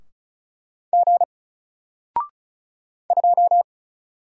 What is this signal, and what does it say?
Morse code, heard as a keyed tone, about 35 wpm, 700 Hz. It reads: G 2
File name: Single-Letter-Number-35wpm